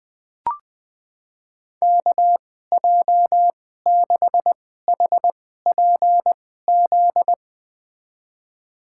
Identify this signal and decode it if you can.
Morse code, heard as a keyed tone, about 21 wpm, spaced as for 15 wpm, 700 Hz. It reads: KJ6HPZ
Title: USA Call Signs - 6 Characters - 2x Character Spacing - 20wpm